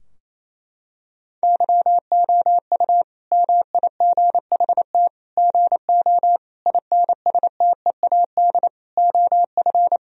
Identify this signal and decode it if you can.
Morse code, heard as a keyed tone, about 28 wpm, 700 Hz. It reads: YOU MIGHT GO INSTEAD OF
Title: Sentences from Top 1000 Words - Review Part 4 - 28wpm